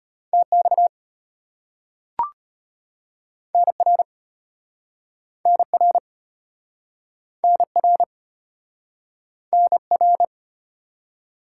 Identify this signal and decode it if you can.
Morse code, heard as a keyed tone, about 33 wpm, 700 Hz. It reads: TX NR NR NR NR